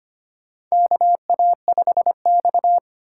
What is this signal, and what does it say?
Morse code, heard as a keyed tone, about 25 wpm, 700 Hz. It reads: KA5X
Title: USA Call Signs - 4 Characters - 3x Character Spacing - 25wpm